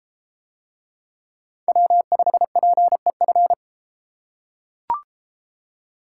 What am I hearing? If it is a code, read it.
Morse code, heard as a keyed tone, about 32 wpm, 700 Hz. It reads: W5PEF